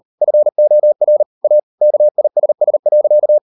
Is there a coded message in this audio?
Yes: Morse code from 0.2 to 3.4 s, heard as a keyed tone, about 38 wpm, 600 Hz. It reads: FOR A KISS.